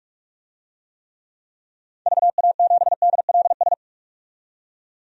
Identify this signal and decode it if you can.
Morse code, heard as a keyed tone, about 42 wpm, 700 Hz. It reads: VA7DLS
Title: DXCC-100 Call Signs - 6 Characters - 45wpm